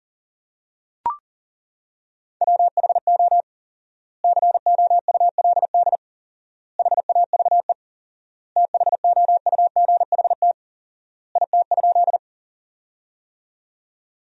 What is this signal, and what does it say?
Morse code, heard as a keyed tone, about 40 wpm, 700 Hz. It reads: WHO COULD HAVE THOUGHT IT?
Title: 4x Word-Spacing - Sentences from Top 300 Words - 40wpm